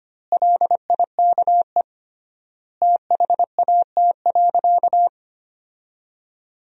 Morse code, heard as a keyed tone, about 25 wpm, 700 Hz. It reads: LIKE THAT.
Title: 3x Word-Spacing - Sentences from Top 300 Words - 25wpm